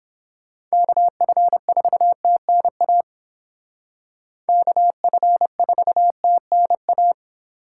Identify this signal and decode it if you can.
Morse code, heard as a keyed tone, about 29 wpm, 700 Hz. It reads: KF4TNA KF4TNA